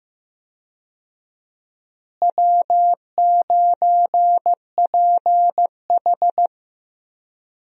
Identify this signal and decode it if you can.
Morse code, heard as a keyed tone, about 15 wpm, 700 Hz. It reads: W9PH
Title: USA Call Signs - 4 Characters - Repeated 3x - 15wpm